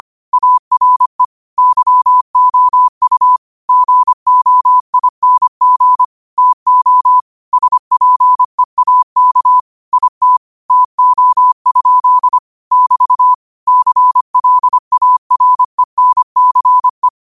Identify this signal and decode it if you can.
Morse code, heard as a keyed tone, about 25 wpm, 1000 Hz. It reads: ARE YOU GOING TO SPEAK IT TO? <BT> CLARENCE
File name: qotdfiles / 25wpm